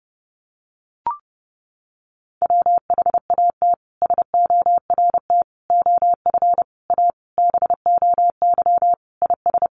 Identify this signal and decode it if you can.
Morse code, heard as a keyed tone, about 30 wpm, 700 Hz. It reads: WHAT SORT OF A BOY IS
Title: Sentences from Top 1300 Words 30wpm